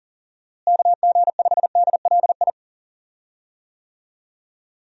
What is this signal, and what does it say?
Morse code, heard as a keyed tone, about 40 wpm, 700 Hz. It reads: KG5DLI
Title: USA Call Signs - 6 Characters - Repeated 3x - 40wpm